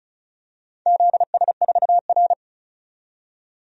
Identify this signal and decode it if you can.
Morse code, heard as a keyed tone, about 34 wpm, 700 Hz. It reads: ZS4R
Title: DXCC-100 Call Signs - 4 Characters - 35wpm